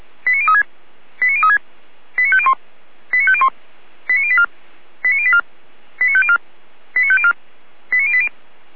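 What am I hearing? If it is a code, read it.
Five-tone selective call (ZVEI), heard as quick runs of five tones, tones of about 70 ms. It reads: 89036 89036 89501 89501 89084 89084 89504 89504 89080
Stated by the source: ZVEI five-tone call received via SDR.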